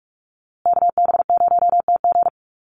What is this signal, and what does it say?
Morse code, heard as a keyed tone, about 44 wpm, 700 Hz. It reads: KB0TG
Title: USA Call Signs Repeated 3x 45wpm